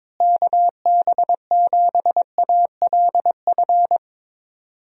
Morse code, heard as a keyed tone, about 22 wpm, 700 Hz. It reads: KB7ALF
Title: USA Call Signs - 6 Characters - Repeated 3x - 22wpm